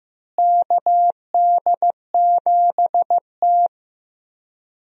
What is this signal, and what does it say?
Morse code, heard as a keyed tone, about 15 wpm, 700 Hz. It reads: KD7T
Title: USA Call Signs - 4 Characters - Repeated 2x - 15wpm